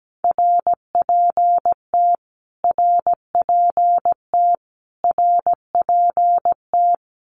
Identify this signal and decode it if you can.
Morse code, heard as a keyed tone, about 17 wpm, 700 Hz. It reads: RPT RPT RPT
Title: Abbreviations Repeated 3x - 17wpm